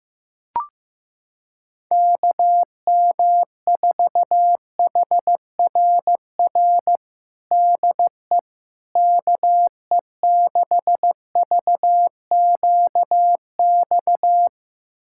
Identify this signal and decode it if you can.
Morse code, heard as a keyed tone, about 15 wpm, 700 Hz. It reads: KM4HRR DE KE6VQX
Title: USA Callsign DE Callsign 15wpm